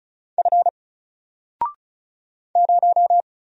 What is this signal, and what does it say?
Morse code, heard as a keyed tone, about 34 wpm, 700 Hz. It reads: F 0
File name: Single-Letter-Number-35wpm